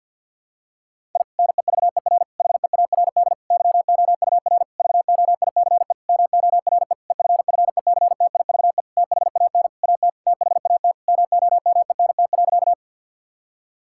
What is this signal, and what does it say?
Morse code, heard as a keyed tone, about 52 wpm, 700 Hz. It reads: I NEVER HEARD YOUR VOICE MORE EFFECTIVE THAN AT THAT MOMENT.
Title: Sentences from Top 1000 Words - Review Part 6 - 50wpm